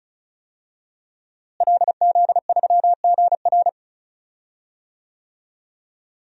Morse code, heard as a keyed tone, about 36 wpm, 700 Hz. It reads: LZ3GR
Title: DXCC-100 Call Signs - 5 Characters - Repeated 3x - 35wpm